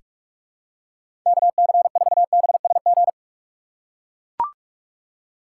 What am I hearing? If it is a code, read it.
Morse code, heard as a keyed tone, about 44 wpm, 700 Hz. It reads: KC4BSG
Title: USA Call Signs - 6 Characters - 6x Character Spacing - 45wpm